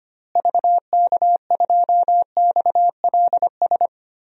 Morse code, heard as a keyed tone, about 25 wpm, 700 Hz. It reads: VK2XLS
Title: DXCC-100 Call Signs - 6 Characters - Repeated 3x - 25wpm